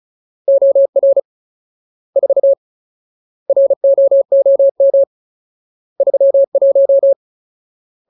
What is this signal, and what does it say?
Morse code, heard as a keyed tone, about 34 wpm, 550 Hz. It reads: OR 4 ROOM 31
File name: Ultimate Word List / 35WPM Ultimate